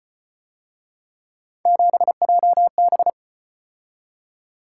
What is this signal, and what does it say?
Morse code, heard as a keyed tone, about 33 wpm, 700 Hz. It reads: 7JB